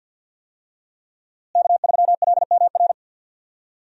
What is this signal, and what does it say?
Morse code, heard as a keyed tone, about 48 wpm, 700 Hz. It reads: K3LMR
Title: USA Call Signs - 5 Characters - Repeated 2x - 50wpm